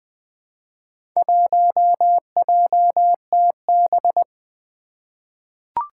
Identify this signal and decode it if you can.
Morse code, heard as a keyed tone, about 20 wpm, 700 Hz. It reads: 1JTB
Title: Four Challenging Letter-Number Combinations 20wpm